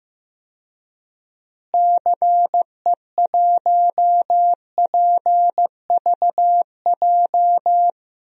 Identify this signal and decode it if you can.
Morse code, heard as a keyed tone, about 15 wpm, 700 Hz. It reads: CE1PVJ
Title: DXCC-100 Call Signs - 6 Characters - Repeated 3x - 15wpm